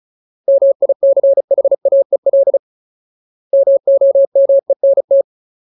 Morse code, heard as a keyed tone, about 35 wpm, 550 Hz. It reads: MICHAEL MOMENT
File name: Ultimate Word List / 35WPM Ultimate